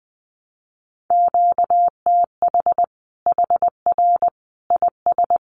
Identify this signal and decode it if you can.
Morse code, heard as a keyed tone, about 20 wpm, 700 Hz. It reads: QTH HR IS
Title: Common QSO Phrases 20wpm